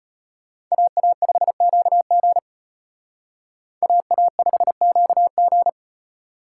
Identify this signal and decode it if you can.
Morse code, heard as a keyed tone, about 32 wpm, 700 Hz. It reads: AA5QG AA5QG